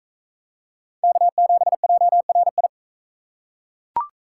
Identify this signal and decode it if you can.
Morse code, heard as a keyed tone, about 41 wpm, 700 Hz. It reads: K7JRI